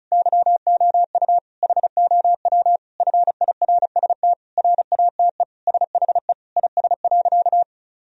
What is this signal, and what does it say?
Morse code, heard as a keyed tone, about 36 wpm, 700 Hz. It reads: YOU HOW FIRST RATE SHE IS.
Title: Sentences from Top 1000 Words - Review Part 4 - 35wpm